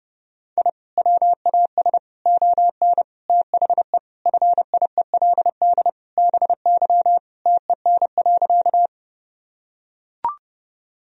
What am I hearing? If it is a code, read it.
Morse code, heard as a keyed tone, about 32 wpm, 700 Hz. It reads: I WAS ON THE FIELD BY TEN.
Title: Sentences from Top 1000 Words - Review Part 2 - 30wpm